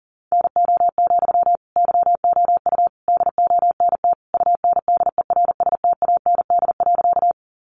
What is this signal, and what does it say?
Morse code, heard as a keyed tone, about 37 wpm, 700 Hz. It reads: NO, YOU DONT UNDERSTAND.
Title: Sentences from Top 600 Words 40wpm